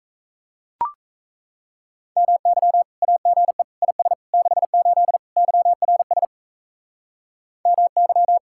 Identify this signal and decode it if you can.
Morse code, heard as a keyed tone, about 34 wpm, 700 Hz. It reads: MY AGE IS 68 YRS MY
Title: Speed-Racing Common QSO Phrases Encore - 28wpm